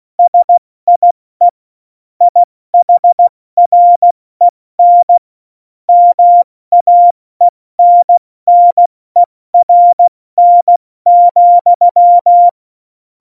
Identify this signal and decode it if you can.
Morse code, heard as a keyed tone, about 16 wpm, 700 Hz. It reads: SIE IHREN MAENNERN,
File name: Morsekurs / Witze / Witz 18 August 2020, 70 F80